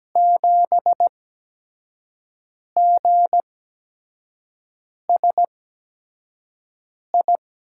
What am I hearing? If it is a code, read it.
Morse code, heard as a keyed tone, about 17 wpm, spaced as for 5 wpm, 700 Hz. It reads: 7GSI
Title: USA Call Signs - 5 Characters - 8x Character Spacing - 17wpm